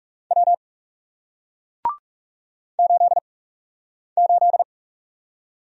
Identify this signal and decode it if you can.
Morse code, heard as a keyed tone, about 34 wpm, 700 Hz. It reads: W 8 8
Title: Speed-Racing Single Letter-Number 30wpm